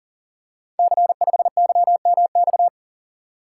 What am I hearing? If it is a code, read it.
Morse code, heard as a keyed tone, about 39 wpm, 700 Hz. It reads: C5YMX
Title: Code Groups - 5 Letter:Numbers - 40wpm